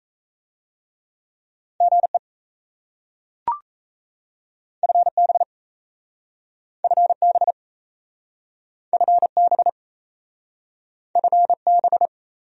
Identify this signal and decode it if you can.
Morse code, heard as a keyed tone, about 37 wpm, 700 Hz. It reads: GE FB FB FB FB